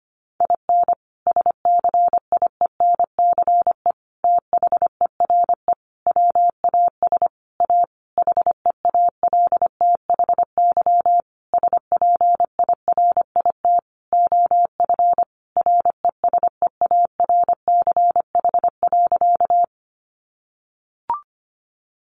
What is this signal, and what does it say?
Morse code, heard as a keyed tone, about 25 wpm, 700 Hz. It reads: IN SCIENCE THERE WAS A HEALTHY SPIRIT OF RESEARCH.
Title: Sentences from Top 1400 Words 25wpm